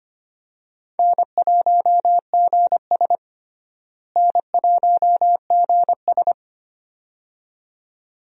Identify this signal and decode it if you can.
Morse code, heard as a keyed tone, about 25 wpm, 700 Hz. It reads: N1GS N1GS
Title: USA Call Signs - 4 Characters - Repeated 2x - 25wpm